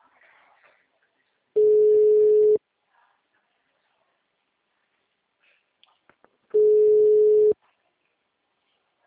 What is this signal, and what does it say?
Ringback tone (Europe, 425 Hz), with speech.
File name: out- -2012